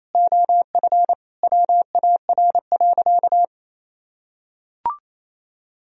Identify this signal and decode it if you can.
Morse code, heard as a keyed tone, about 28 wpm, 700 Hz. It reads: OF WAR.
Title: Sentences from Top 1000 Words - Review Part 3 - 28wpm